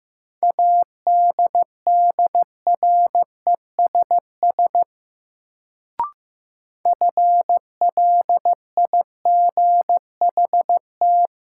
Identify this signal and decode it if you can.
Morse code, heard as a keyed tone, about 15 wpm, 700 Hz. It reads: ADDRESS FLIGHT
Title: Sentences from Top 1000 Words - New Word Drill - 15wpm